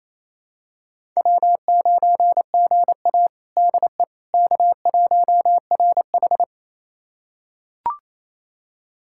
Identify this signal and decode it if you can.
Morse code, heard as a keyed tone, about 28 wpm, 700 Hz. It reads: W9GA DE K1RH
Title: USA Callsign DE Callsign - 4 Characters - 28wpm